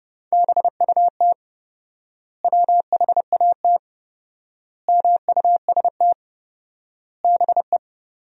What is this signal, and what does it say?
Morse code, heard as a keyed tone, about 28 wpm, 700 Hz. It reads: BUT WHAT MUST BE
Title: 4x Word-Spacing - Sentences from Top 500 Words - 30wpm